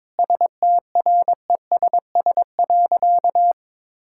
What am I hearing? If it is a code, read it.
Morse code, heard as a keyed tone, about 22 wpm, 700 Hz. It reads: STRESS.
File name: Difficult Sentences with Hints - 22wpm